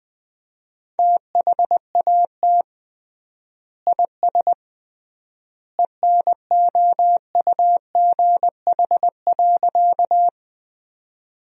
Morse code, heard as a keyed tone, about 20 wpm, 700 Hz. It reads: THAT IS ENOUGH.
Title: 3x Word-Spacing - Sentences from Top 400 Words - 20wpm